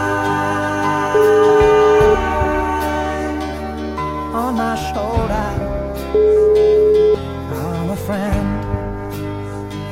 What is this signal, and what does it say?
Ringback tone (Europe, 425 Hz), with noise.